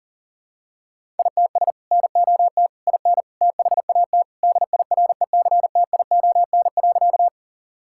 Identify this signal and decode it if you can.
Morse code, heard as a keyed tone, about 40 wpm, 700 Hz. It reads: ITS NOT IN THAT DIRECTION.
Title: Sentences from Top 1000 Words - Review Part 5 - 40wpm